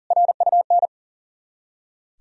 Morse code, heard as a keyed tone, about 51 wpm, 700 Hz. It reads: RUN